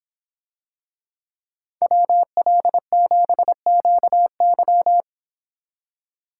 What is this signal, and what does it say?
Morse code, heard as a keyed tone, about 24 wpm, 700 Hz. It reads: WL7QY